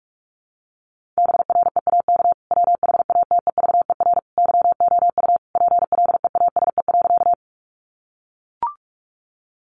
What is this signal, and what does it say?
Morse code, heard as a keyed tone, about 44 wpm, 700 Hz. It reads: BREAK WHATEVER YOU PLEASE.
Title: Sentences from Top 1500 Words 45wpm